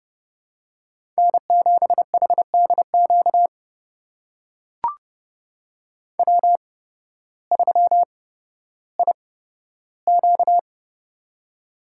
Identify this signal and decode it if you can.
Morse code, heard as a keyed tone, about 30 wpm, spaced as for 14 wpm, 700 Hz. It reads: N7HDQ W3IQ
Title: USA Call Signs - 8x Character Spacing - 30wpm